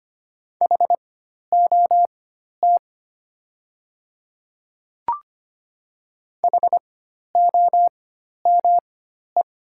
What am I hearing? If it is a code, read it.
Morse code, heard as a keyed tone, about 25 wpm, spaced as for 12 wpm, 700 Hz. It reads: HOT HOME
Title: 4x Character Spacing - Top 200 Words - 25wpm